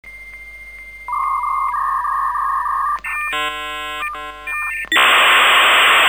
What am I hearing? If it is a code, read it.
Dial-up modem handshake.